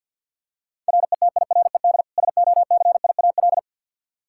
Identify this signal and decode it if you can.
Morse code, heard as a keyed tone, about 49 wpm, 700 Hz. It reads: RETIRED SOCIAL
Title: Practice QSO - Exchange 3 - More Info - 50wpm